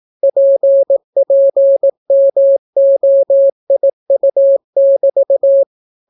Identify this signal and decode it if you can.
Morse code, heard as a keyed tone, about 18 wpm, 550 Hz. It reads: PPMOIU<BT>